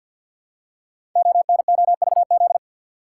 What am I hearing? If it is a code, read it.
Morse code, heard as a keyed tone, about 52 wpm, 700 Hz. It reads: ONOVZ
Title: Code Groups - 5 Letters - 50wpm